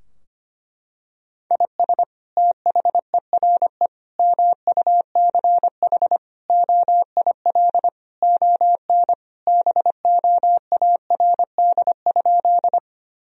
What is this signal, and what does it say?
Morse code, heard as a keyed tone, about 25 wpm, 700 Hz. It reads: IS THERE MUCH OIL ON BOARD?
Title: Sentences from Top 1000 Words - Review Part 4 - 25wpm